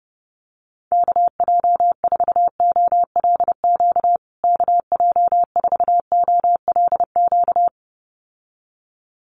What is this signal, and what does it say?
Morse code, heard as a keyed tone, about 30 wpm, 700 Hz. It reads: KJ4OLQ KJ4OLQ
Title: USA Call Signs Repeated 2x 30wpm